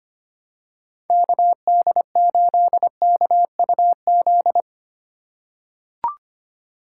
Morse code, heard as a keyed tone, about 25 wpm, 700 Hz. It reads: KD8KUZ
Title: USA Call Signs - 6 Characters - 3x Character Spacing - 25wpm